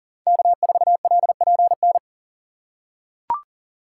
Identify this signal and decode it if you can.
Morse code, heard as a keyed tone, about 40 wpm, 700 Hz. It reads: K4LPN
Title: USA Call Signs - 5 Characters - 6x Character Spacing - 40wpm